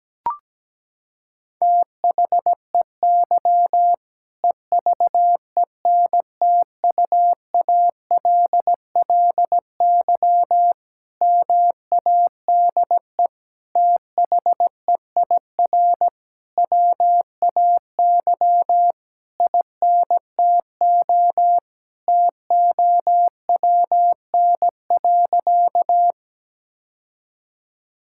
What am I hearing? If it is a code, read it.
Morse code, heard as a keyed tone, about 17 wpm, 700 Hz. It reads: THEY EVENTUALLY MADE THEIR WAY INTO TOWN.
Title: Sentences from Top 1000 Words - Review Part 6 - 17wpm